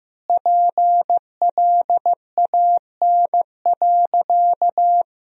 Morse code, heard as a keyed tone, about 15 wpm, 700 Hz. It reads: PLAN.
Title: 3x Word-Spacing - Sentences from Top 500 Words - 15wpm